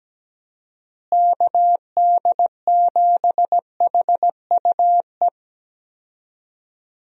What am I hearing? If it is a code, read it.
Morse code, heard as a keyed tone, about 17 wpm, 700 Hz. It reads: KD7HUE
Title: USA Call Signs - 6 Characters - Repeated 3x - 17wpm